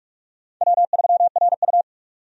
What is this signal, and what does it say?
Morse code, heard as a keyed tone, about 46 wpm, 700 Hz. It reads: W3RU
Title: USA Call Signs - 4 Characters - 45wpm